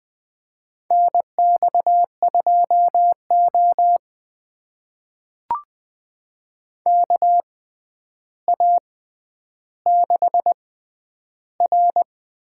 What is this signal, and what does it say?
Morse code, heard as a keyed tone, about 20 wpm, spaced as for 10 wpm, 700 Hz. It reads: NX2O KA6R
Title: USA Call Signs - 4 Characters - 6x Character Spacing - 20wpm